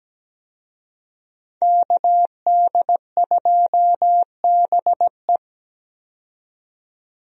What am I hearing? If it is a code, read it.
Morse code, heard as a keyed tone, about 17 wpm, 700 Hz. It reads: KD2BE
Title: USA Call Signs - 5 Characters - 17wpm